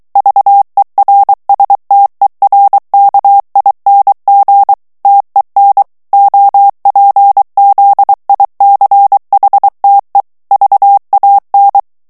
Morse code, heard as a keyed tone, about 23 wpm, 800 Hz. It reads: VERSTERKING TEN OPZICHTE VAN
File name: HOLLANDAIS texte clair 1 vitesse 22 mots minute